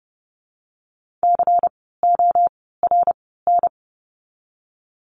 Morse code, heard as a keyed tone, about 30 wpm, 700 Hz. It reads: C O R N
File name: Lesson 16f - Letter c - Words 30wpm